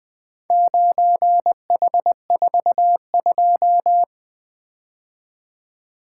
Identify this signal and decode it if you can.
Morse code, heard as a keyed tone, about 20 wpm, 700 Hz. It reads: 9H42
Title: Four Letter-Number Combinations - 20wpm